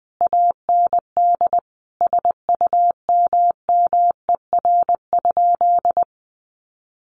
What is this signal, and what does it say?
Morse code, heard as a keyed tone, about 19 wpm, 700 Hz. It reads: AND SUMMER?
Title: Sentences from Top 1200 Words 20wpm